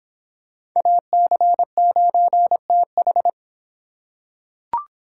Morse code, heard as a keyed tone, about 24 wpm, 700 Hz. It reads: AC9TH